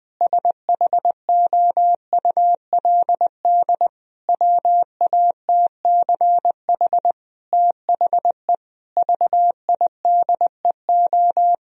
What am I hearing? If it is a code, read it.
Morse code, heard as a keyed tone, about 20 wpm, 700 Hz. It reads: SHOULD WATCH THE VIDEO
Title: Sentences from Top 1000 Words - Review Part 4 - 20wpm